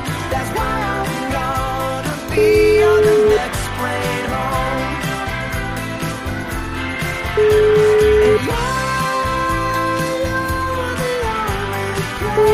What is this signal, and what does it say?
Ringback tone (Europe, 425 Hz), with noise.